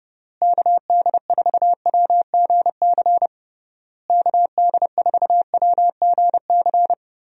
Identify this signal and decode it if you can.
Morse code, heard as a keyed tone, about 30 wpm, 700 Hz. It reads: KD4WGC KD4WGC
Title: USA Call Signs - 6 Characters - Repeated 3x - 30wpm